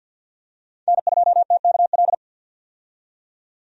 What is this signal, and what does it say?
Morse code, heard as a keyed tone, about 50 wpm, 700 Hz. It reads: N2TKL
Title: USA Call Signs - 5 Characters - 50wpm